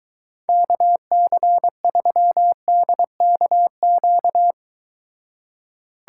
Morse code, heard as a keyed tone, about 23 wpm, 700 Hz. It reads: KC3DKQ